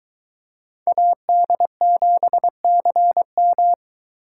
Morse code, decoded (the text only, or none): AD7CM